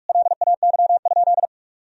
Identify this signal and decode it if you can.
Morse code, heard as a keyed tone, about 42 wpm, 700 Hz. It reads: LAY?
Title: Sentences from Top 400 Words and Abbreviations 45wpm